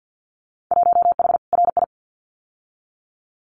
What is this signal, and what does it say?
Morse code, heard as a keyed tone, about 50 wpm, 700 Hz. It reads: 1H RI
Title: Winter Field Day Exchanges 50wpm